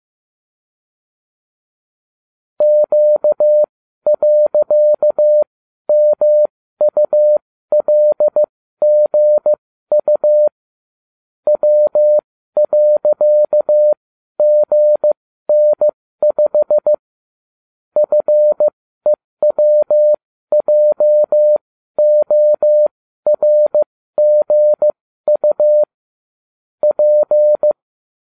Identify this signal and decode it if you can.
Morse code, heard as a keyed tone, about 15 wpm, spaced as for 12 wpm, 600 Hz. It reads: Q.MULGU W.GN5 FEWJORGU P